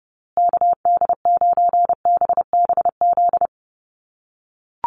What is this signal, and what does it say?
Morse code, heard as a keyed tone, about 30 wpm, 700 Hz. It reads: KD9BBZ
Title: USA Call Signs Repeated 3x 30wpm